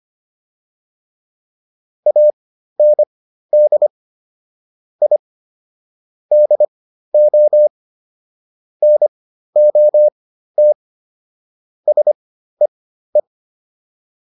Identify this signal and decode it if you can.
Morse code, heard as a keyed tone, about 25 wpm, spaced as for 13 wpm, 600 Hz. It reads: AND I DO NOT SEE